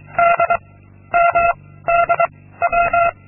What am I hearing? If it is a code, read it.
Morse code, heard as a keyed tone, about 23 wpm, 690 Hz. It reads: D M D W
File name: Linksmakalnis-2